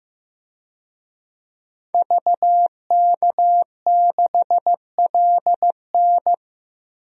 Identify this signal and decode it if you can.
Morse code, heard as a keyed tone, about 15 wpm, 700 Hz. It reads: VK6LN
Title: DXCC-100 Call Signs - 5 Characters - Repeated 3x - 15wpm